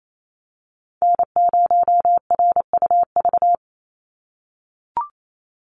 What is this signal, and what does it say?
Morse code, heard as a keyed tone, about 28 wpm, 700 Hz. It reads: N0RUV
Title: USA Call Signs 28wpm